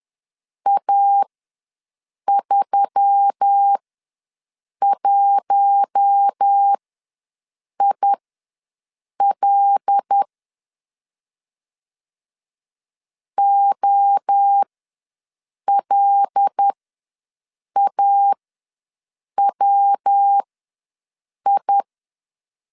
Morse code, heard as a keyed tone, about 10 wpm, spaced as for 6 wpm, 800 Hz. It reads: A31IL OLAWI
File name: LEZIONI / 5 LBVC / 5 L ws 10wpm 3636